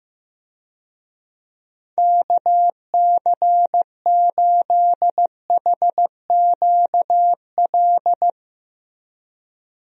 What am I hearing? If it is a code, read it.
Morse code, heard as a keyed tone, about 15 wpm, 700 Hz. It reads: KC8HQL
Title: USA Call Signs - 6 Characters - Repeated 2x - 15wpm